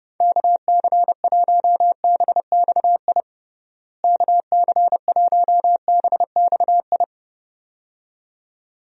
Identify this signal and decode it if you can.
Morse code, heard as a keyed tone, about 30 wpm, 700 Hz. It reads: KC1BXI KC1BXI
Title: USA Call Signs - 6 Characters - Repeated 2x - 30wpm